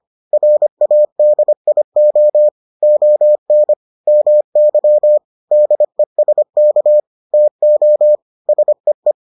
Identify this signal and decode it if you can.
Morse code, heard as a keyed tone, about 25 wpm, 600 Hz. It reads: RADIO ON MY DESK TO SEE